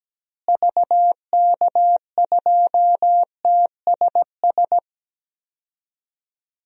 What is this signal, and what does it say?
Morse code, heard as a keyed tone, about 17 wpm, 700 Hz. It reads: VK2TSS
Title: DXCC-100 Call Signs - 6 Characters - Repeated 2x - 17wpm